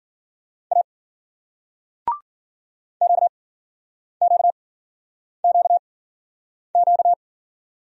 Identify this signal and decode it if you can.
Morse code, heard as a keyed tone, about 52 wpm, spaced as for 9 wpm, 700 Hz. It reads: A QQQQ